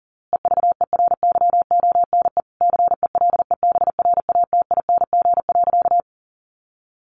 Morse code, heard as a keyed tone, about 40 wpm, 700 Hz. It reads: EVERYONE CELEBRATING.
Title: A Sound Of Thunder 40wpm